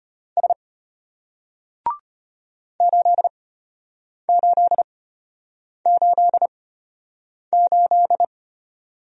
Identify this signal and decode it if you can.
Morse code, heard as a keyed tone, about 34 wpm, spaced as for 8 wpm, 700 Hz. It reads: S 8888